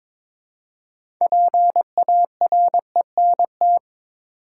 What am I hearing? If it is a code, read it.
Morse code, heard as a keyed tone, about 22 wpm, 700 Hz. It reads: PARENT
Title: Six Letter Words - 22wpm